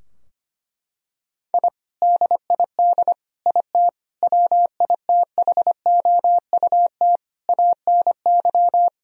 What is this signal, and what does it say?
Morse code, heard as a keyed tone, about 25 wpm, 700 Hz. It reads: I DID IT WITHOUT ANY
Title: Sentences from Top 1000 Words - Review Part 3 - 25wpm